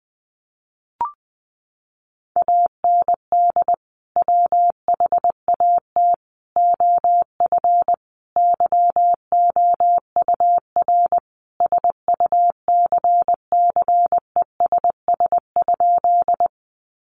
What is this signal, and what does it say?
Morse code, heard as a keyed tone, about 19 wpm, 700 Hz. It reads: AND WHAT OF YOUR SUCCESS?
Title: Sentences from Top 900 Words 20wpm